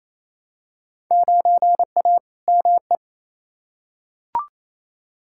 Morse code, heard as a keyed tone, about 28 wpm, 700 Hz. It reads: 9A ME
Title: Field Day Exchanges Encore - 28wpm